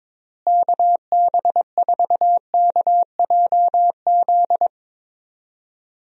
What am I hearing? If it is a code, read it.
Morse code, heard as a keyed tone, about 23 wpm, 700 Hz. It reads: KB4KJZ
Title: USA Call Signs - 6 Characters - Repeated 3x - 22wpm